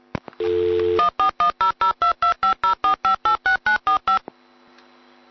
DTMF, heard as telephone touch tones, tones of about 120 ms. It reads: 1410022504586945